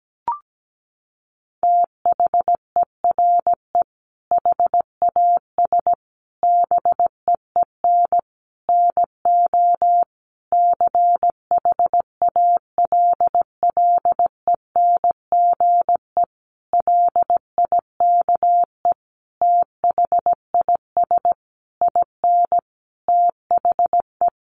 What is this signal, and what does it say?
Morse code, heard as a keyed tone, about 17 wpm, 700 Hz. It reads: THERE HAS BEEN NO CHALLENGE LIKE THIS IN THE
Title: Sentences from Top 1500 Words 17wpm